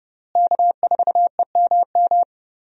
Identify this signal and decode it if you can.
Morse code, heard as a keyed tone, about 30 wpm, 700 Hz. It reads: K4EMM
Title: USA Call Signs - 5 Characters - Repeated 2x - 30wpm